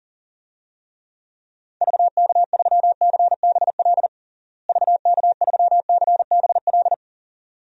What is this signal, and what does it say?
Morse code, heard as a keyed tone, about 37 wpm, 700 Hz. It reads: VK3CBL VK3CBL
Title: DXCC-100 Call Signs - 6 Characters - Repeated 2x - 40wpm